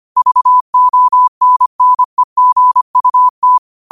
Morse code, heard as a keyed tone, about 25 wpm, 1000 Hz. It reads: VONNEGUT
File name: qotdfiles / 25wpm